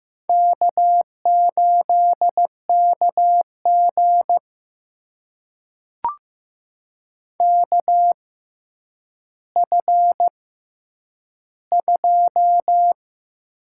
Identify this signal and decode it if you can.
Morse code, heard as a keyed tone, about 15 wpm, spaced as for 9 wpm, 700 Hz. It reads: K8KG KF2
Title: USA Call Signs - 4 Characters - 6x Character Spacing - 15wpm